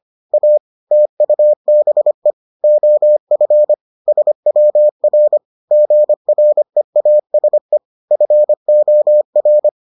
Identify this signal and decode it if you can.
Morse code, heard as a keyed tone, about 25 wpm, 600 Hz. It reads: A TUBE OF SWR GREASE FOR